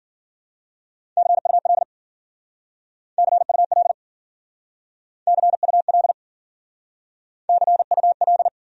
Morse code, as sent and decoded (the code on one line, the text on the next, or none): -.-. ..- .-.. / -.-. ..- .-.. / -.-. ..- .-.. / -.-. ..- .-..
CUL CUL CUL CUL